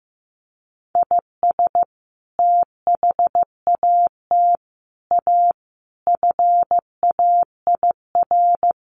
Morse code, heard as a keyed tone, about 15 wpm, 700 Hz. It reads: IS THAT A FAIR
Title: Sentences from Top 1300 Words 15wpm